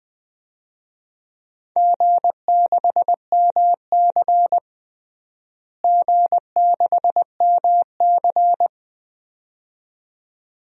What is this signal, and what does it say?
Morse code, heard as a keyed tone, about 21 wpm, 700 Hz. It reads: G6MC G6MC
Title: DXCC-100 Call Signs - 4 Characters - Repeated 2x - 20wpm